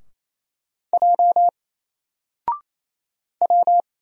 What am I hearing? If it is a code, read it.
Morse code, heard as a keyed tone, about 27 wpm, 700 Hz. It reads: J W
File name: Single-Letter-Number-28wpm